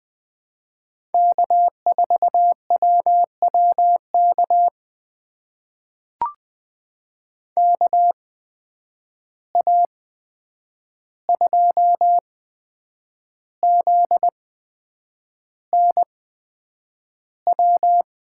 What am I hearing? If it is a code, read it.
Morse code, heard as a keyed tone, about 20 wpm, spaced as for 8 wpm, 700 Hz. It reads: K4WWK KA2ZNW
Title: USA Call Signs - 8x Character Spacing - 20wpm